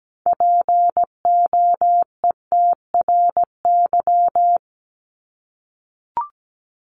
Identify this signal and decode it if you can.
Morse code, heard as a keyed tone, about 17 wpm, 700 Hz. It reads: POETRY
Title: Top 1751-2000 Words 17wpm